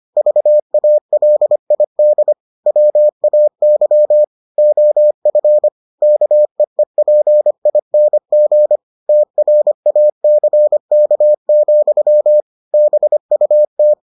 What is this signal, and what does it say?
Morse code, heard as a keyed tone, about 25 wpm, 600 Hz. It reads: VALID WAY OF KEEPING TRACK, BUT